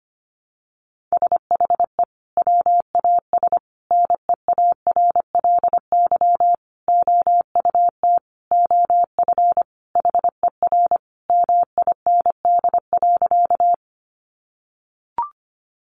Morse code, heard as a keyed tone, about 25 wpm, 700 Hz. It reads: SHE WAS NEARLY OUT OF HER MIND.
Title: Sentences from Top 1300 Words 25wpm